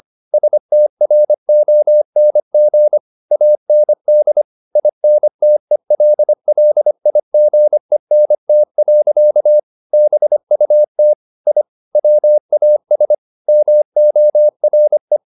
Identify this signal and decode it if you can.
Morse code, heard as a keyed tone, about 25 wpm, 600 Hz. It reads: STRONG AND INTELLIGENT. BUT I WAS MORE